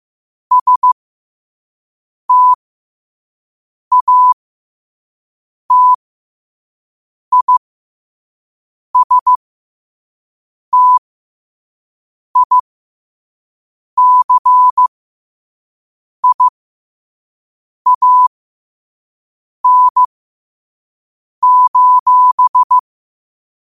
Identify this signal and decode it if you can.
Morse code, heard as a keyed tone, about 15 wpm, spaced as for 5 wpm, 1000 Hz. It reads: STATISTICIAN:
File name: qotdfiles / 5wpm